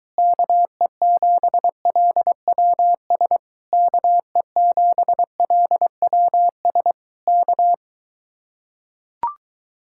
Morse code, heard as a keyed tone, about 23 wpm, 700 Hz. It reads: KE7LWS KE7LWS K